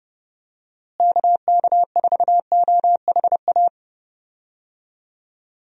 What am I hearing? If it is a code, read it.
Morse code, heard as a keyed tone, about 31 wpm, 700 Hz. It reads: KK4OHA
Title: USA Call Signs - 6 Characters - Repeated 3x - 30wpm